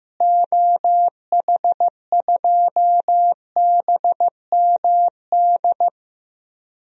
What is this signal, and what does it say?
Morse code, heard as a keyed tone, about 15 wpm, 700 Hz. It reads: OH2BMD
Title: DXCC-100 Call Signs - 6 Characters - Repeated 3x - 15wpm